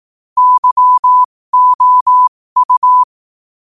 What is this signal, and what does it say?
Morse code, heard as a keyed tone, about 18 wpm, spaced as for 15 wpm, 1000 Hz. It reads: YOU
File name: qotdfiles / 15wpm